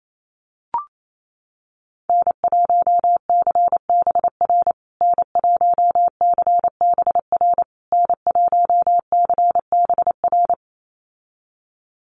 Morse code, heard as a keyed tone, about 28 wpm, 700 Hz. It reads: N1CBR N1CBR N1CBR
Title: USA Call Signs Repeated 3x 28wpm